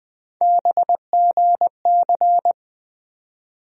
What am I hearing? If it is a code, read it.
Morse code, heard as a keyed tone, about 20 wpm, 700 Hz. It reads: BGC
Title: USA Call Sign Postfixes - 3 Letters 20wpm